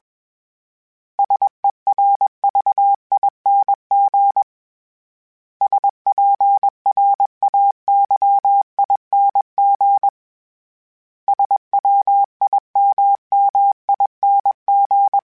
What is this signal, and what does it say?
Morse code, heard as a keyed tone, about 21 wpm, 800 Hz. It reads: SERVING SPRAYING SWIMMING